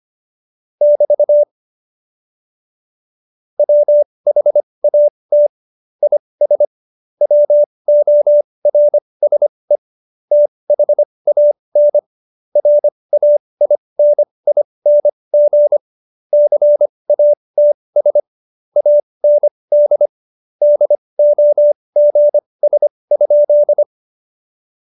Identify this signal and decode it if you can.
Morse code, heard as a keyed tone, about 25 wpm, spaced as for 20 wpm, 590 Hz. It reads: <BT> WHAT IS WORSE THAN RAINING CATS AND DOGS?
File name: Jokes-25-char-20-eff